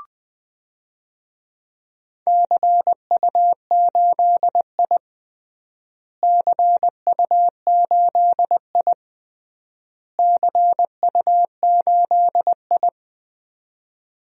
Morse code, heard as a keyed tone, about 20 wpm, 700 Hz. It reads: CU8I CU8I CU8I
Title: DXCC-100 Call Signs - 4 Characters - Repeated 3x - 20wpm